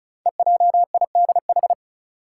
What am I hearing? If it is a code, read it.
Morse code, heard as a keyed tone, about 36 wpm, 690 Hz. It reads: EJIDH
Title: Code Groups - 5 Letters - 35wpm